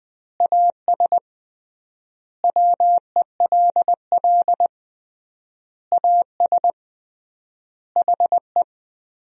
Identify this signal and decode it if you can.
Morse code, heard as a keyed tone, about 21 wpm, 700 Hz. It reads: AS WELL AS HE
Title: 3x Word-Spacing - Sentences from Top 500 Words - 20wpm